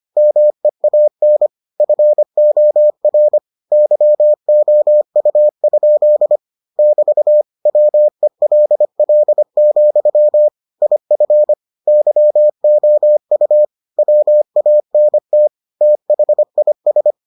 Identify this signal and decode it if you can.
Morse code, heard as a keyed tone, about 25 wpm, 600 Hz. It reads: MEAN FOR YOU? <BT> WELL, IF YOU WANT THIS